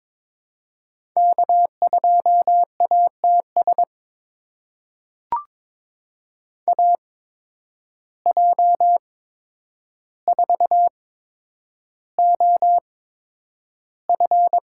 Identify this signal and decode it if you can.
Morse code, heard as a keyed tone, about 22 wpm, spaced as for 9 wpm, 700 Hz. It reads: K2ATS AJ4OF
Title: USA Call Signs - 5 Characters - 8x Character Spacing - 22wpm